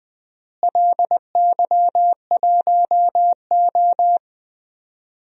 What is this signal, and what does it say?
Morse code, heard as a keyed tone, about 20 wpm, 700 Hz. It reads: LY1O
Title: DXCC-100 Call Signs - 4 Characters - Repeated 3x - 20wpm